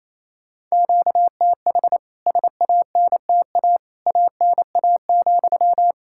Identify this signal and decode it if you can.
Morse code, heard as a keyed tone, about 28 wpm, 700 Hz. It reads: QTH SANTA ANA,
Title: K1USN SST QSO Practice 28wpm